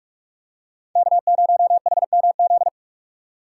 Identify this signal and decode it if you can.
Morse code, heard as a keyed tone, about 45 wpm, 700 Hz. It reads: K0HMZ
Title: K1USN SST QSO Practice - 45wpm